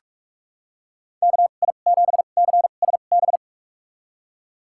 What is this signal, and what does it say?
Morse code, heard as a keyed tone, about 46 wpm, spaced as for 32 wpm, 700 Hz. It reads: KI7CSB
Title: USA Call Signs - 6 Characters - 2x Character Spacing - 45wpm